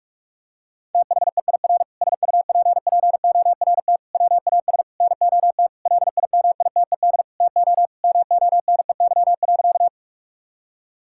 Morse code, heard as a keyed tone, about 46 wpm, 700 Hz. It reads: THEIR SUPPORT WAS NOT LIMITED TO MONEY.
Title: Sentences from Top 1000 Words - Review Part 6 - 45wpm